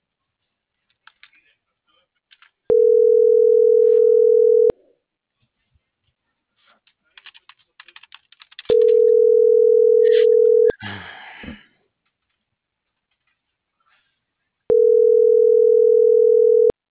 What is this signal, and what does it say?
Ringback tone (North America), 440 + 480 Hz.
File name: CENTRAL MABE / out antonio.garcia - .91787